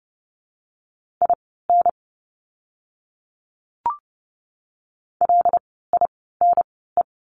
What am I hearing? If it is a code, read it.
Morse code, heard as a keyed tone, about 30 wpm, spaced as for 17 wpm, 700 Hz. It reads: IN LINE
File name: course / 30:10wpm / 115 Lesson 19f - Letter w - Words 30wpm